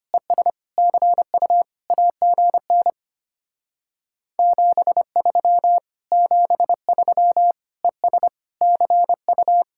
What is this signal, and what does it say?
Morse code, heard as a keyed tone, about 32 wpm, 700 Hz. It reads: ES CU AGN 73 73 ES CU